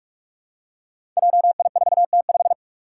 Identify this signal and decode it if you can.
Morse code, heard as a keyed tone, about 46 wpm, 700 Hz. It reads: JI4T5
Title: Five Challenging Letter-Number Combinations 45wpm